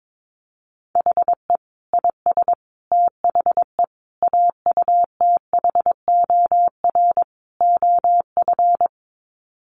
Morse code, heard as a keyed tone, about 22 wpm, 700 Hz. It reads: HE IS THE AUTHOR OF
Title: Sets of 5 Words Encore2 22wpm